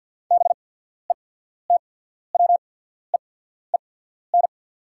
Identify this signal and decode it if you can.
Morse code, heard as a keyed tone, about 48 wpm, spaced as for 14 wpm, 690 Hz. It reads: BETWEEN
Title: 8x Character Spacing - Top 500 Words - 50wpm